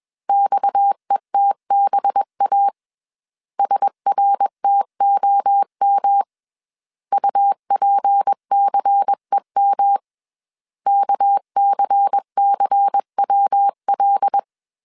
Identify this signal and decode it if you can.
Morse code, heard as a keyed tone, about 21 wpm, 800 Hz. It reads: XETBA SRTOM UPCEM KCCWL
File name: CW / DET / 1 6 1500C S A 20wpm 2048